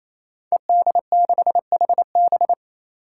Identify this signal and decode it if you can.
Morse code, heard as a keyed tone, about 27 wpm, 700 Hz. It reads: ED6HB
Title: Five Challenging Letter-Number Combinations 28wpm